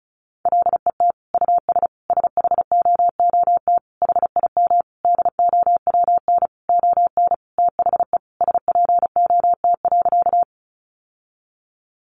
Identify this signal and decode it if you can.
Morse code, heard as a keyed tone, about 34 wpm, 700 Hz. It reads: LET US SHOOT HIM DOWN ON THE SPOT.
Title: Sentences from Top 1200 Words 35wpm